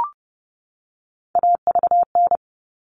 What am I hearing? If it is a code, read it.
Morse code, heard as a keyed tone, about 30 wpm, 700 Hz. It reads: AVN
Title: All Cut Numbers - 3-Digit 30wpm